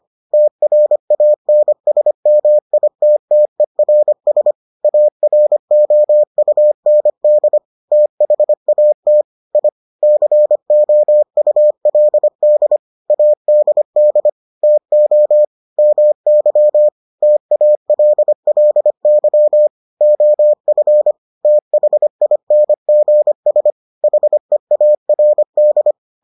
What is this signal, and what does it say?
Morse code, heard as a keyed tone, about 25 wpm, 600 Hz. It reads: TRANSMITTERS AROUND THAT I COULD ADD TO MY TALLY OF THINGS HEARD